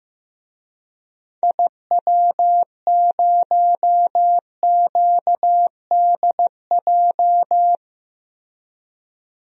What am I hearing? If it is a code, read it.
Morse code, heard as a keyed tone, about 15 wpm, 700 Hz. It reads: IW0QDJ
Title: DXCC-100 Call Signs - 6 Characters - 15wpm